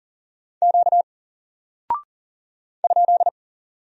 Morse code, heard as a keyed tone, about 40 wpm, 700 Hz. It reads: Q ?
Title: Single Character (Letter:Number:Punctuation:Prosign) Alt 40wpm